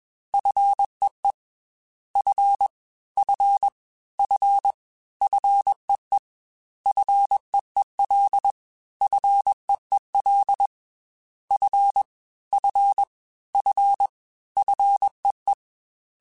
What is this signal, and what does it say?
Morse code, heard as a keyed tone, about 21 wpm, 800 Hz. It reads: FEE F F F FEE FEEL FEEL F F F FEE
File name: Fs / Single F mp3s / Single F Prefixes 3x2 Twice @20